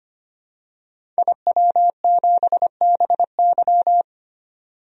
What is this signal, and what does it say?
Morse code, heard as a keyed tone, about 25 wpm, 700 Hz. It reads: IW7BY
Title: DXCC-100 Call Signs - 5 Characters - 25wpm